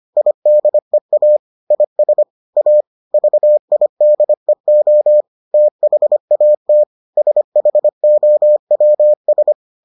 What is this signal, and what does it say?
Morse code, heard as a keyed tone, about 25 wpm, 600 Hz. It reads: IDEA IS A VIDEO THAT SHOWS